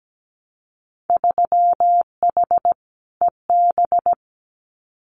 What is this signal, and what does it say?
Morse code, heard as a keyed tone, about 17 wpm, 700 Hz. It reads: 3H EB
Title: Winter Field Day Exchanges Encore 17wpm